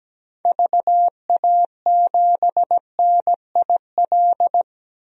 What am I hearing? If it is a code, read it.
Morse code, heard as a keyed tone, about 17 wpm, 700 Hz. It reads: VA7NIL
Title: DXCC-100 Call Signs - 6 Characters - 17wpm